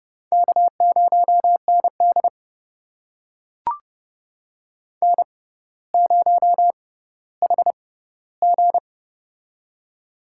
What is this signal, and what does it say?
Morse code, heard as a keyed tone, about 30 wpm, spaced as for 15 wpm, 700 Hz. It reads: K0ND N0HG
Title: USA Call Signs - 4 Characters - 6x Character Spacing - 30wpm